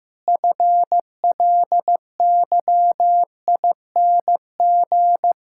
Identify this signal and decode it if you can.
Morse code, heard as a keyed tone, about 15 wpm, 700 Hz. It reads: FLYING
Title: Pangram Challenge v2 - 15wpm